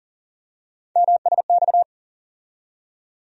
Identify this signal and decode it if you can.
Morse code, heard as a keyed tone, about 40 wpm, 700 Hz. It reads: MSX
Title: USA Call Sign Postfixes - 3 Letters 40wpm